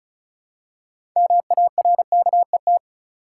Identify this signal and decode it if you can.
Morse code, heard as a keyed tone, about 36 wpm, 700 Hz. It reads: MARKET
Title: Six Letter Words - 35wpm